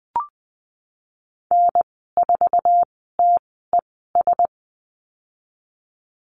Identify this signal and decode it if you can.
Morse code, heard as a keyed tone, about 20 wpm, spaced as for 15 wpm, 700 Hz. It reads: N4TES
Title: 069 Lesson 12g - Letter d - Call Signs 20wpm